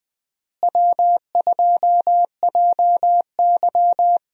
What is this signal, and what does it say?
Morse code, heard as a keyed tone, about 20 wpm, 700 Hz. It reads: W2JY
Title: USA Call Signs - 4 Characters - Repeated 3x - 20wpm